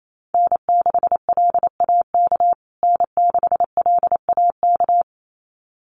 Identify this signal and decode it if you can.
Morse code, heard as a keyed tone, about 28 wpm, 700 Hz. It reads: N6LAK N6LAK
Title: USA Call Signs Repeated 2x 28wpm